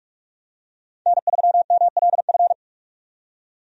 Morse code, heard as a keyed tone, about 45 wpm, 700 Hz. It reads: N3MLF
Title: USA Call Signs - 5 Characters - 45wpm